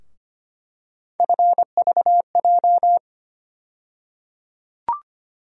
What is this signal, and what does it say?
Morse code, heard as a keyed tone, about 25 wpm, 700 Hz. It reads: FVJ
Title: USA Call Sign Postfixes - 3 Letters 25wpm